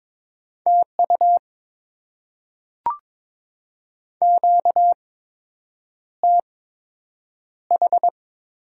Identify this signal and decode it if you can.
Morse code, heard as a keyed tone, about 22 wpm, spaced as for 8 wpm, 700 Hz. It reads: TU QTH
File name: Top 20 QSO Elements - 8x Character Spacing - 22wpm